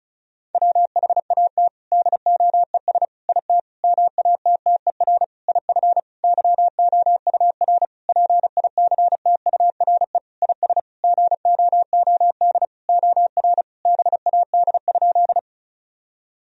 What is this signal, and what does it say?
Morse code, heard as a keyed tone, about 36 wpm, 700 Hz. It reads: WHAT DOES IT MATTER IF YOUR PICTURE IS GOOD OR BAD?
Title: Sentences from Top 1000 Words - Review Part 1 - 35wpm